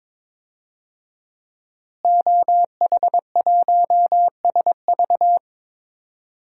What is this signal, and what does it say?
Morse code, heard as a keyed tone, about 22 wpm, 700 Hz. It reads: OH1SV
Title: DXCC-100 Call Signs - 5 Characters - Repeated 2x - 22wpm